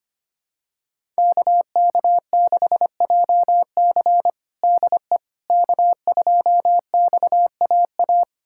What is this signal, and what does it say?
Morse code, heard as a keyed tone, about 25 wpm, 700 Hz. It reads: KK6JC DE K2XAA
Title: USA Callsign DE Callsign - 5 Characters - 25wpm